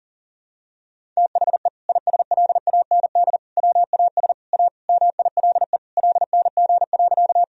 Morse code, heard as a keyed tone, about 40 wpm, 700 Hz. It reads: THE ISLAND WAS A MILE LNG.
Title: Sentences from Top 500 Words and Abbreviations 40wpm